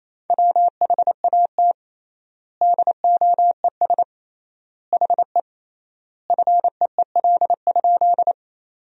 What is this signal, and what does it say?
Morse code, heard as a keyed tone, about 28 wpm, 700 Hz. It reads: WHAT DOES HE FEEL?
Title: 3x Word-Spacing - Sentences from Top 400 Words - 28wpm